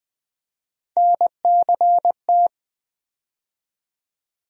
Morse code, heard as a keyed tone, about 20 wpm, 700 Hz. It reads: NCT
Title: USA Call Sign Postfixes - 3 Letters 20wpm